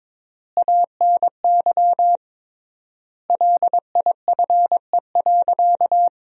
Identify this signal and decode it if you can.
Morse code, heard as a keyed tone, about 22 wpm, 700 Hz. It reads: ANY LIFE.
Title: 3x Word-Spacing - Sentences from Top 300 Words - 22wpm